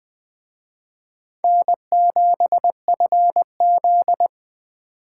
Morse code, heard as a keyed tone, about 21 wpm, 700 Hz. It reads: N7FZ
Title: Speed Racing - USA 1x2 Call Signs 17wpm